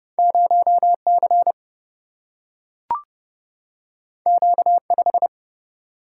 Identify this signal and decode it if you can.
Morse code, heard as a keyed tone, about 30 wpm, 700 Hz. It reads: 0 C Q 5
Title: Single-Letter Single-Number 30wpm